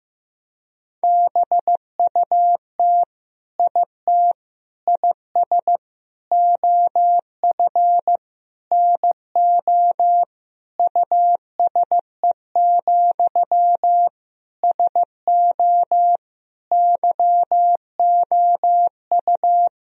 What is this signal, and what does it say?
Morse code, heard as a keyed tone, about 15 wpm, 700 Hz. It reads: BUT IT IS OF NO USE, SO YOU
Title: Sentences from Top 1000 Words - Review Part 2 - 15wpm